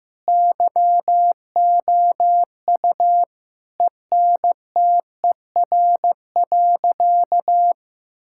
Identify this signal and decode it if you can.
Morse code, heard as a keyed tone, about 15 wpm, 700 Hz. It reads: YOU ENTER.
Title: Sentences from Top 1600 Words 15wpm